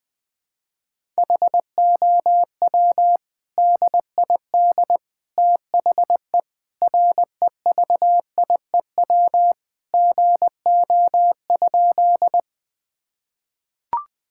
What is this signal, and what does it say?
Morse code, heard as a keyed tone, about 20 wpm, 700 Hz. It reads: HOW DID THE REVIEW GO?
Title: Sentences from Top 1000 Words - Review Part 6 - 20wpm